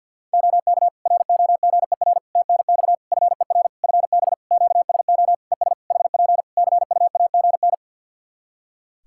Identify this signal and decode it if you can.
Morse code, heard as a keyed tone, about 50 wpm, 700 Hz. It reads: OK ROGER TNX FER FB QSO ES HP CUAGN
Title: Practice QSO - Exchange 4 - Ending - 50wpm